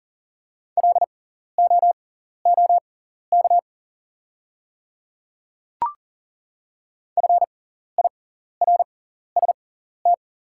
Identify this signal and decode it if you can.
Morse code, heard as a keyed tone, about 40 wpm, spaced as for 14 wpm, 690 Hz. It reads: LOOK FIRST
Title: 6x Character Spacing - Top 100 Words - 40wpm